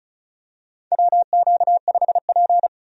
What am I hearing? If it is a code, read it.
Morse code, heard as a keyed tone, about 36 wpm, 700 Hz. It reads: WQ5P
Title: USA Call Signs - 4 Characters - 35wpm